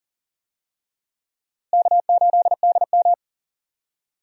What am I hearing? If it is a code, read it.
Morse code, heard as a keyed tone, about 37 wpm, 700 Hz. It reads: K8DM
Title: USA Call Signs - 4 Characters - Repeated 2x - 40wpm